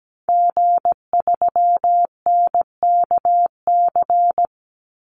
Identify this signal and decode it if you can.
Morse code, heard as a keyed tone, about 17 wpm, 700 Hz. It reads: G3NKC
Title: Call Signs Repeated 2x - 17wpm